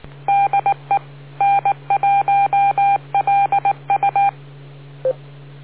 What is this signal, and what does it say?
Morse code, heard as a keyed tone, about 19 wpm, 800 Hz. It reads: DE N1LU